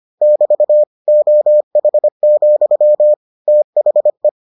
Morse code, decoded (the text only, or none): <BT> OH, THE